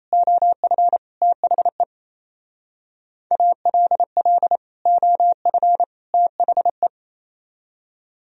Morse code, heard as a keyed tone, about 34 wpm, 700 Hz. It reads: OF THE ALL OF THE
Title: Speed-Racing - Sets of 3 Words 28wpm